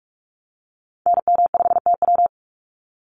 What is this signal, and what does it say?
Morse code, heard as a keyed tone, about 44 wpm, 700 Hz. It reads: NM5TW
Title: NA SOTA Activator Callsigns 45wpm